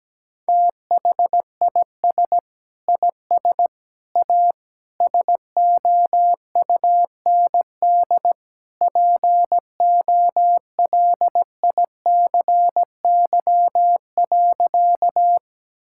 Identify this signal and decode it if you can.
Morse code, heard as a keyed tone, about 17 wpm, 700 Hz. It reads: THIS IS A SOUND POLICY.
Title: Sentences from Top 1000 Words - Review Part 5 - 17wpm